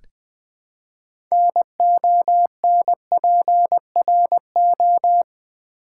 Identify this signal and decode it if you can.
Morse code, heard as a keyed tone, about 20 wpm, 700 Hz. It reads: NONPRO
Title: Build A Word - 3 Letters At A Time - 9-Letter Words - 20wpm